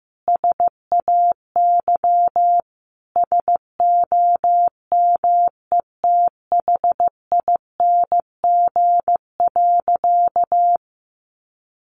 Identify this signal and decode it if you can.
Morse code, heard as a keyed tone, about 15 wpm, 700 Hz. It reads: SAY SOMETHING.
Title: Sentences from Top 1200 Words 15wpm